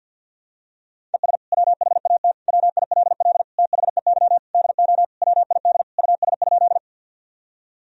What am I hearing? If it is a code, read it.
Morse code, heard as a keyed tone, about 48 wpm, 700 Hz. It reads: ES WHAT WILL THEY DO WID US?
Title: Sentences from Top 200 Words and Abbreviations 50wpm